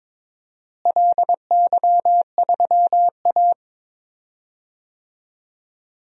Morse code, heard as a keyed tone, about 22 wpm, 700 Hz. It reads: LY3A
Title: DXCC-100 Call Signs - 4 Characters - Repeated 2x - 22wpm